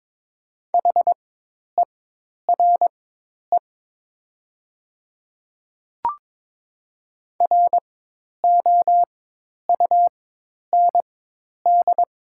Morse code, heard as a keyed tone, about 22 wpm, spaced as for 10 wpm, 700 Hz. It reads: HERE ROUND
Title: 4x Character Spacing - Top 300 Words - 22wpm